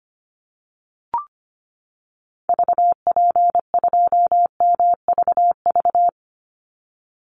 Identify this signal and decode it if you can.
Morse code, heard as a keyed tone, about 25 wpm, 700 Hz. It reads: VP2MVV
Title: Call Signs - 25wpm